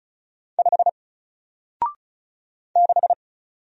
Morse code, heard as a keyed tone, about 33 wpm, 700 Hz. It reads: 5 6
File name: Single-Letter-Number-35wpm